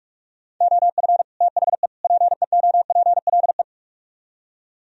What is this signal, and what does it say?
Morse code, heard as a keyed tone, about 47 wpm, 700 Hz. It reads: OF THE PEOPLE